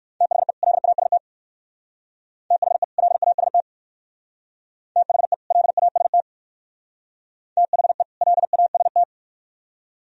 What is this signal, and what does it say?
Morse code, heard as a keyed tone, about 57 wpm, 700 Hz. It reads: THE LAST THE LAST THE LAST THE LAST